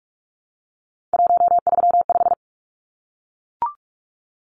Morse code, heard as a keyed tone, about 44 wpm, 700 Hz. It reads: 135
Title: Numbers - 3-Digit 45wpm